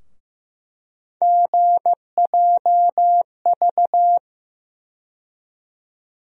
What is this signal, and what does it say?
Morse code, heard as a keyed tone, about 15 wpm, 700 Hz. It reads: GJV
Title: USA Call Sign Postfixes - 3 Letters 15wpm